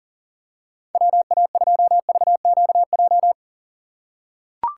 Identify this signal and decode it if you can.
Morse code, heard as a keyed tone, about 39 wpm, 700 Hz. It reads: WA2VQJ
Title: USA Call Signs - 6 Characters - 2x Character Spacing - 40wpm